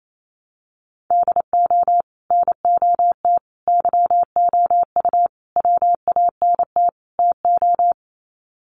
Morse code, heard as a keyed tone, about 28 wpm, 700 Hz. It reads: DO NOT YOU WANT TO
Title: Sets of 5 Words Encore2 28wpm